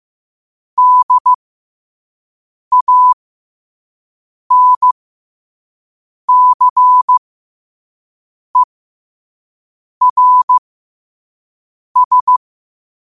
Morse code, heard as a keyed tone, about 15 wpm, spaced as for 5 wpm, 1000 Hz. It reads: DANCERS
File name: qotdfiles / 5wpm